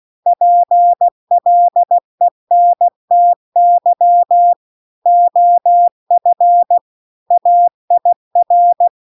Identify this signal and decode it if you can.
Morse code, heard as a keyed tone, about 16 wpm, 700 Hz. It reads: PLENTY OF AIR
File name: CW-The-Big-Twoheaded-River-2-16WPM